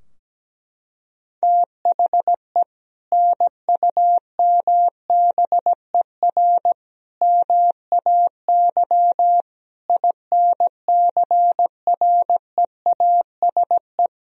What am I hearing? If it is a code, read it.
Morse code, heard as a keyed tone, about 17 wpm, 700 Hz. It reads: THE NUMBER MAY INCREASE
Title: Sentences from Top 1000 Words - Review Part 4 - 17wpm